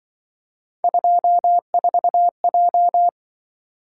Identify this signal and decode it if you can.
Morse code, heard as a keyed tone, about 24 wpm, 700 Hz. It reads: 24J
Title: Speed-Racing - Three Letter-Number Combinations - 20wpm